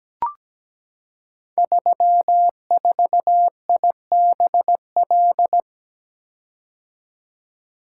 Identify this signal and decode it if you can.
Morse code, heard as a keyed tone, about 17 wpm, 700 Hz. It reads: 34IBL
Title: Code Groups - 5 Letter:Numbers - 17wpm